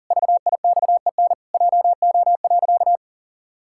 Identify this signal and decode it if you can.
Morse code, heard as a keyed tone, about 40 wpm, 700 Hz. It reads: VIXEN JO.
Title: Pangram Challenge v2 - 40wpm